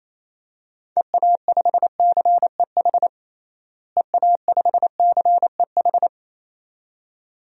Morse code, heard as a keyed tone, about 27 wpm, 700 Hz. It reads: EA5CEH EA5CEH
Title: DXCC-100 Call Signs - 6 Characters - Repeated 2x - 28wpm